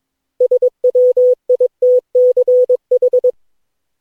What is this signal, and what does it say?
Morse code, heard as a keyed tone, about 22 wpm, 500 Hz. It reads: SWITCH
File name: cw / Session9 cw training